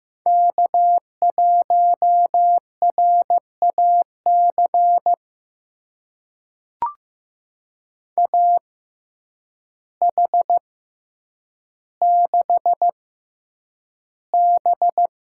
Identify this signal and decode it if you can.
Morse code, heard as a keyed tone, about 15 wpm, spaced as for 8 wpm, 700 Hz. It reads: K1RAC AH6B
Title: USA Call Signs - 6x Character Spacing - 15wpm